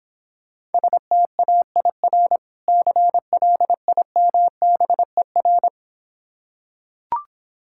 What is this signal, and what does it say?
Morse code, heard as a keyed tone, about 26 wpm, 700 Hz. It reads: STAIR CLIMBER